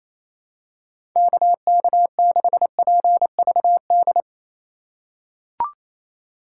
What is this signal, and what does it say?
Morse code, heard as a keyed tone, about 28 wpm, 700 Hz. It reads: KK6PVD
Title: USA Call Signs - 6 Characters - 2x Character Spacing - 28wpm